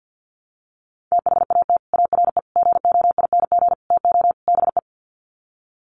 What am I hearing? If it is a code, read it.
Morse code, heard as a keyed tone, about 48 wpm, 700 Hz. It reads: THAT ARE GOING TO BE
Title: Sets of 5 Words Encore2 50wpm